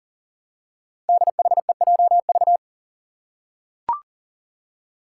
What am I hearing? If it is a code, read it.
Morse code, heard as a keyed tone, about 40 wpm, 700 Hz. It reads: DHEJV
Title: Five Challenging Letter-Number Combinations 40wpm